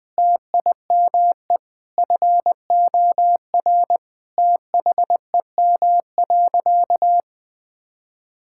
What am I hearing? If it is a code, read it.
Morse code, heard as a keyed tone, about 20 wpm, 700 Hz. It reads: TIME FOR THEM.
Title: Difficult Sentences with Hints - 20wpm